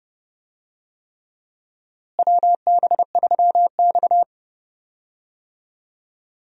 Morse code, heard as a keyed tone, about 30 wpm, 700 Hz. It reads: WB3X
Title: USA Call Signs - 4 Characters - Repeated 3x - 30wpm